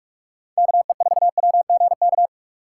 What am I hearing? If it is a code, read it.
Morse code, heard as a keyed tone, about 47 wpm, 700 Hz. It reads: KE4WGK
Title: USA Call Signs - 6 Characters - 3x Character Spacing - 45wpm